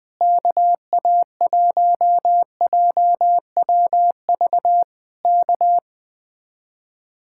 Morse code, heard as a keyed tone, about 20 wpm, 700 Hz. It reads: KA1JWV K
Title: Speed-Racing Common QSO Phrases - 17wpm